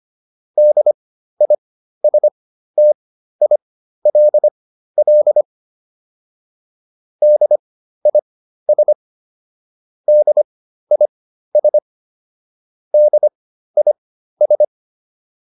Morse code, heard as a keyed tone, about 25 wpm, spaced as for 13 wpm, 600 Hz. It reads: DISTILL DIS DIS DIS